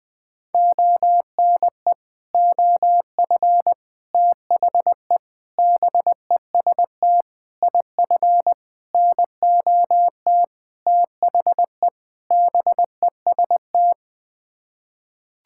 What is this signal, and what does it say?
Morse code, heard as a keyed tone, about 20 wpm, 700 Hz. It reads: ONE OF THE BEST IF NOT THE BEST
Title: Sets of 8 Words - 20wpm